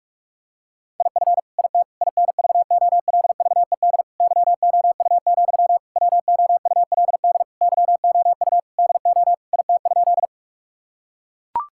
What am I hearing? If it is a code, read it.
Morse code, heard as a keyed tone, about 41 wpm, 700 Hz. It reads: IF IT INVOLVED YOU, WOULD YOU DO IT?
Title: Sentences from Top 1000 Words - Review Part 5 - 45wpm